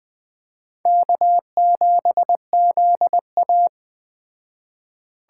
Morse code, heard as a keyed tone, about 20 wpm, 700 Hz. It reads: K7ZA